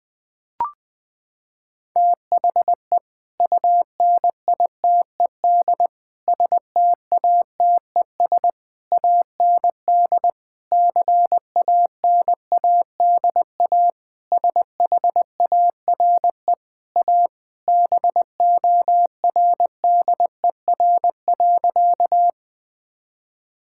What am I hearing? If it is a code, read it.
Morse code, heard as a keyed tone, about 20 wpm, 700 Hz. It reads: THE UNITED STATES AND CANADA SHARE A BORDER.
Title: Sentences with 5-Word Phrases - 20wpm